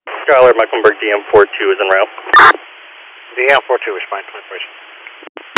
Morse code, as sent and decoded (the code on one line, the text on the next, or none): none